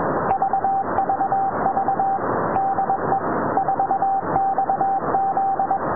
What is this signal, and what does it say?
Morse code, heard as a keyed tone, about 21 wpm, 780 Hz, with noise. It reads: VVV DE 4XZ